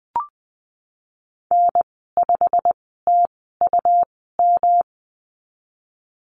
Morse code, heard as a keyed tone, about 19 wpm, spaced as for 14 wpm, 700 Hz. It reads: N5TUM
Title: 140 Lesson 23g - Letter f - Call Signs 20wpm